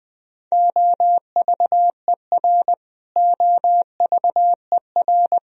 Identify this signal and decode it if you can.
Morse code, heard as a keyed tone, about 20 wpm, 700 Hz. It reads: OVER OVER
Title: Top 100 Words Repeated 20wpm